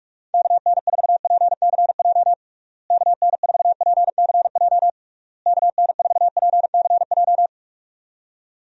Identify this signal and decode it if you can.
Morse code, heard as a keyed tone, about 41 wpm, 700 Hz. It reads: KN4PCJ KN4PCJ KN4PCJ
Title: USA Call Signs - 6 Characters - Repeated 3x - 45wpm